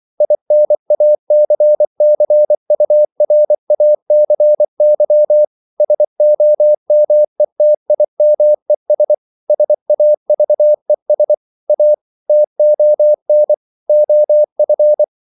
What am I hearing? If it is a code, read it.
Morse code, heard as a keyed tone, about 24 wpm, 600 Hz. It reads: INACCURACY SOMETIMES SAVES A TON OF
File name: CW / Practice / EnglishQuotations0224